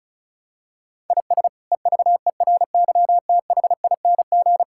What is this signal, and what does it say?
Morse code, heard as a keyed tone, about 36 wpm, 700 Hz. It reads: IS EVERYTHING
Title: Sentences from Top 1000 Words - Review Part 6 - 35wpm